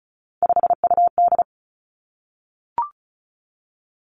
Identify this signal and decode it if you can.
Morse code, heard as a keyed tone, about 36 wpm, 700 Hz. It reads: 5UD
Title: Three Challenging Letter-Number Combinations 35wpm